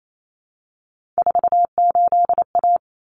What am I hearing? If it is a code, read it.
Morse code, heard as a keyed tone, about 28 wpm, 700 Hz. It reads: 48A
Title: Common Cut Numbers - 3-Digit 28wpm